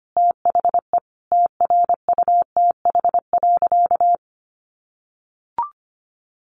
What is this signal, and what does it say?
Morse code, heard as a keyed tone, about 25 wpm, 700 Hz. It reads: THE TRUTH.
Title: Sentences from Top 1100 Words 25wpm